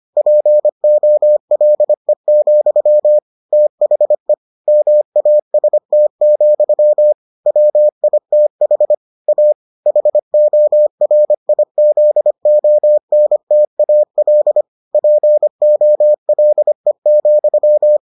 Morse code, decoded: POLE, THE MAST, WITH A HORIZONTAL POLE,

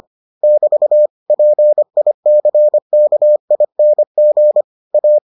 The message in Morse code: <BT> PICKING A

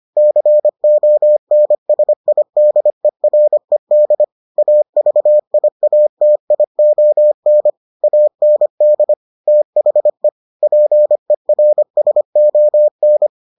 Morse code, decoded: CONSIDERED AVIATION AND THE PERSON

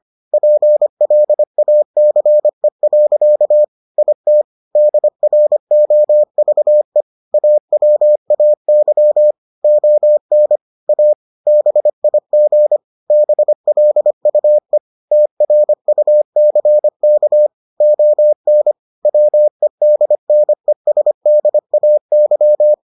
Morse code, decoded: PLACE. IT DROVE AWAY ON A BIG BLUE TRUCK ON WEDNESDAY

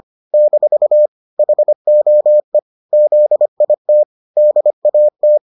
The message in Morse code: - HOE ZIT DAT

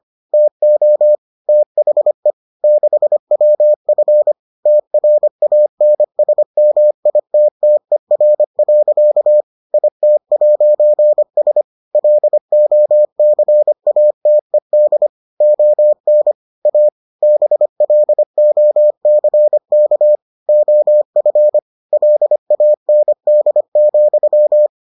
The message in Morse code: TO THE 6WF TRANSMITTER. IT'S LOCATED ON A BLOCK OF LAND,

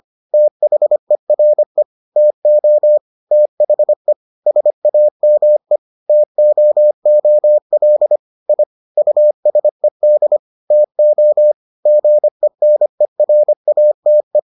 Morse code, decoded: THERE TO THE SAME TOOL I USED TO GENERATE

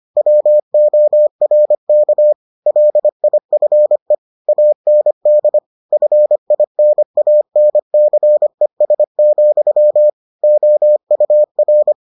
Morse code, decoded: WORK LIFE AND FINANCES, OUR

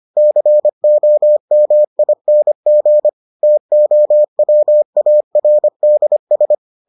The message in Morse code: COMING TOWARDS

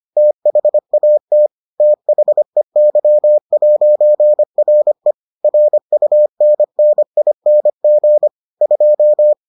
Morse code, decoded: THAT THEY'RE RUNNING 2

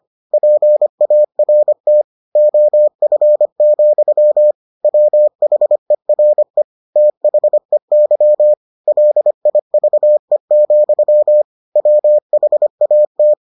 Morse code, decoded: PART OF, WHERE THEY LIVE, WHAT

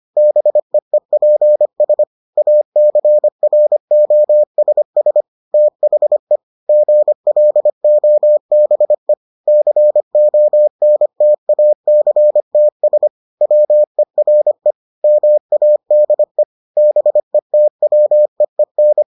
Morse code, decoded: BEEPS ACROSS THE GLOBE CONTACTS WERE MADE BETWEEN

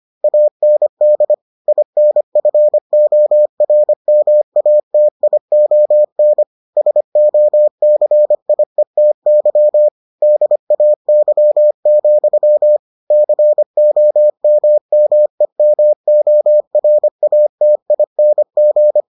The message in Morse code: AND INFORMATION SOCIETY DAY, COMMEMORATING

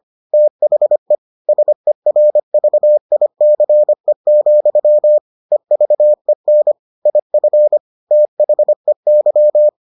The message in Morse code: THE SERVICE, EVEN IF THEY

